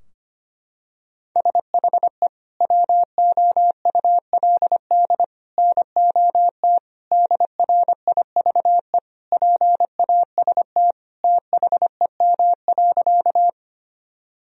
Morse code, heard as a keyed tone, about 25 wpm, 700 Hz. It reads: SHE WOULD NOT DRIVE PAST THEM.
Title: Sentences from Top 1000 Words - Review Part 3 - 25wpm